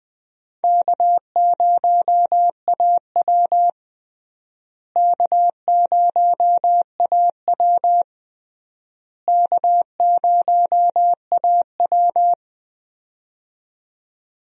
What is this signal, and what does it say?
Morse code, heard as a keyed tone, about 21 wpm, 700 Hz. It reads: K0AW K0AW K0AW
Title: USA Call Signs - 4 Characters - Repeated 3x - 20wpm